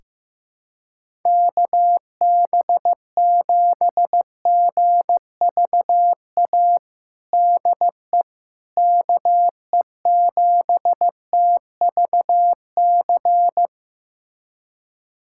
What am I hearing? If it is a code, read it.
Morse code, heard as a keyed tone, about 15 wpm, 700 Hz. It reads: KB7GVA DE KE7TVC
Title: USA Callsign DE Callsign - 6 Characters - 15wpm